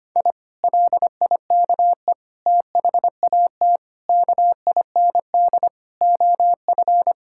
Morse code, heard as a keyed tone, about 25 wpm, 700 Hz. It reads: I LIKE THAT KIND OF
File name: practice / 25wpm / Sentences with 3-Word Phrases - 25wpm